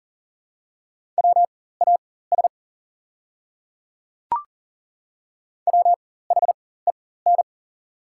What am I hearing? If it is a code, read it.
Morse code, heard as a keyed tone, about 40 wpm, spaced as for 19 wpm, 700 Hz. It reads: WAS WHEN
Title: 4x Character Spacing - Top 100 Words - 40wpm